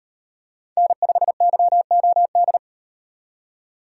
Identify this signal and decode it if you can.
Morse code, heard as a keyed tone, about 38 wpm, 700 Hz. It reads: N5YOD